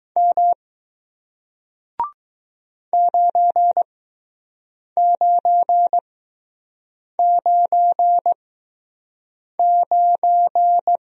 Morse code, heard as a keyed tone, about 20 wpm, spaced as for 6 wpm, 700 Hz. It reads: M 9999